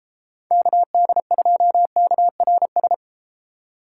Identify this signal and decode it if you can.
Morse code, heard as a keyed tone, about 38 wpm, 700 Hz. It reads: KD2KRS